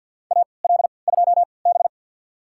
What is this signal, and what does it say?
Morse code, heard as a keyed tone, about 51 wpm, spaced as for 28 wpm, 700 Hz. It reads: AL2B
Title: USA Call Signs - 4 Characters - 3x Character Spacing - 50wpm